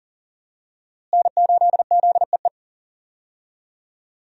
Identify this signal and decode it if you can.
Morse code, heard as a keyed tone, about 40 wpm, 700 Hz. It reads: N8ZEE
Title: USA Call Signs - 5 Characters - Repeated 3x - 40wpm